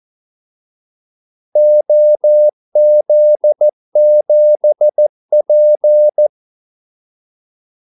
Morse code, heard as a keyed tone, about 14 wpm, 600 Hz. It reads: OZ7P
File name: Callsigns-14wpm